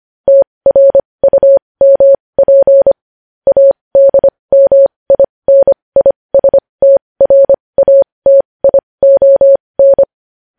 Morse code, heard as a keyed tone, about 25 wpm, spaced as for 20 wpm, 550 Hz. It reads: TRUMP ADMINISTRATION
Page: news-headlines-in-morse-code-at-20-wpm